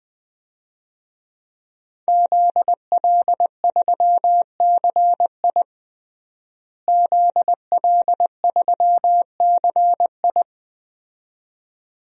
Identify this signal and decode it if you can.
Morse code, heard as a keyed tone, about 20 wpm, 700 Hz. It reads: ZL3CI ZL3CI
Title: DXCC-100 Call Signs - 5 Characters - Repeated 2x - 20wpm